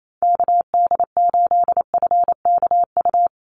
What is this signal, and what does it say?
Morse code, heard as a keyed tone, about 28 wpm, 700 Hz. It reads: KD8FKU
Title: USA Call Signs 28wpm